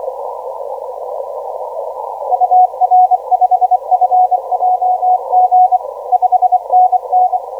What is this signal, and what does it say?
Morse code, heard as a keyed tone, about 24 wpm, 730 Hz. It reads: UR5FJG 5NN
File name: TM28UFT 0905UTC 12m